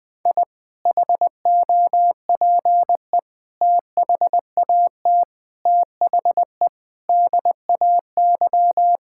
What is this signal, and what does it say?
Morse code, heard as a keyed tone, about 20 wpm, 700 Hz. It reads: I HOPE THAT THE DAY
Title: Sentences with 2-Word Phrases - Spoken Beginning - 20wpm